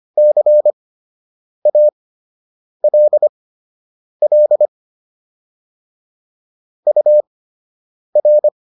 Morse code, heard as a keyed tone, about 25 wpm, spaced as for 8 wpm, 600 Hz. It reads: CALL UR